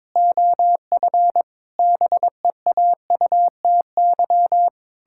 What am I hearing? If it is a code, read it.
Morse code, heard as a keyed tone, about 22 wpm, 700 Hz. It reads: OF BEAUTY
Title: Pangram Challenge v2 - 22wpm